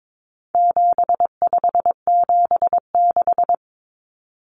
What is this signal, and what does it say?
Morse code, heard as a keyed tone, about 22 wpm, 700 Hz. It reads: 7576
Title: Common Cut Numbers - 4-Digit 22wpm